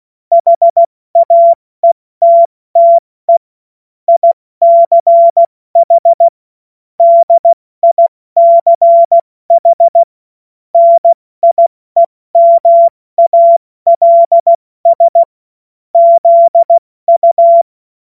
Morse code, heard as a keyed tone, about 16 wpm, 680 Hz. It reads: HAETTE ICH DICH NIEMALS ZU
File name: Morsekurs / Witze / Witz 05 März 2024 70 F80